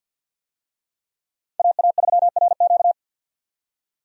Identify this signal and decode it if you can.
Morse code, heard as a keyed tone, about 51 wpm, 700 Hz. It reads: AA3RQ
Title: USA Call Signs - 5 Characters - 50wpm